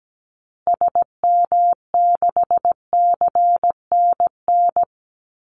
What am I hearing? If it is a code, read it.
Morse code, heard as a keyed tone, about 17 wpm, 700 Hz. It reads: SM6CNN
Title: Call Signs Repeated 2x - 17wpm